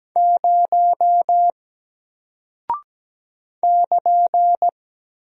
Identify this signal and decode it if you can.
Morse code, heard as a keyed tone, about 17 wpm, 700 Hz. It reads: 0 <KN>
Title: Single Character (Letter:Number:Punctuation:Prosign) Alt 17wpm